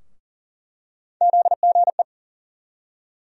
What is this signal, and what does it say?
Morse code, heard as a keyed tone, about 40 wpm, 700 Hz. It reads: ZGE
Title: USA Call Sign Postfixes - 3 Letters 40wpm